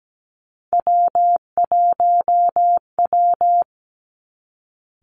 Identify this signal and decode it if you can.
Morse code, heard as a keyed tone, about 17 wpm, 700 Hz. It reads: W1W
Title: Three Challenging Letter-Number Combinations 17wpm